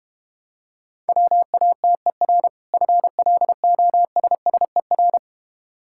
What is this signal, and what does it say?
Morse code, heard as a keyed tone, about 31 wpm, 700 Hz. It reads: WATER FLOSSER
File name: practice / 25wpm / Difficult Words - Speed-Racing with Hints - 25wpm